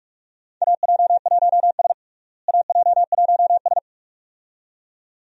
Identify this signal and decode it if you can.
Morse code, heard as a keyed tone, about 41 wpm, 700 Hz. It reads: AJ1S AJ1S
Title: USA Call Signs - 4 Characters - Repeated 2x - 45wpm